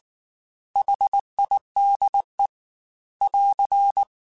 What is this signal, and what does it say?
Morse code, heard as a keyed tone, about 19 wpm, 800 Hz. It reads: HIDE <AR>